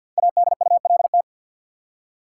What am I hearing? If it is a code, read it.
Morse code, heard as a keyed tone, about 49 wpm, 700 Hz. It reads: ADULT
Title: Five Letter Words - 50wpm